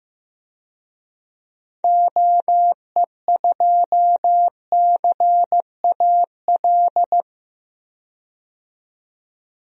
Morse code, heard as a keyed tone, about 15 wpm, 700 Hz. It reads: OE2CAL
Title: DXCC-100 Call Signs - 6 Characters - 15wpm